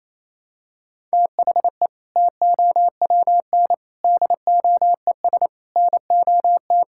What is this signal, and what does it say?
Morse code, heard as a keyed tone, about 28 wpm, 700 Hz. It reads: THE TOWN DOES NOT
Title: Sentences from Top 1000 Words - Review Part 5 - 28wpm